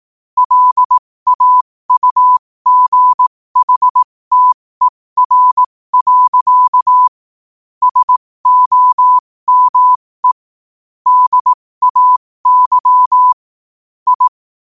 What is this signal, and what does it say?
Morse code, heard as a keyed tone, about 18 wpm, spaced as for 15 wpm, 1000 Hz. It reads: LAUGHTER. SOME DAY I
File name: qotdfiles / 15wpm